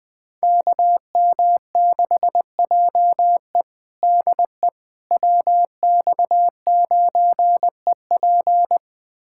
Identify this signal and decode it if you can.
Morse code, heard as a keyed tone, about 21 wpm, 700 Hz. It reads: KM6JE DE WX9EP
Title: USA Callsign DE Callsign 20wpm